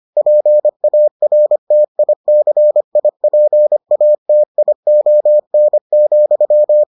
Morse code, heard as a keyed tone, about 25 wpm, 600 Hz. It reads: PARTICIPATION,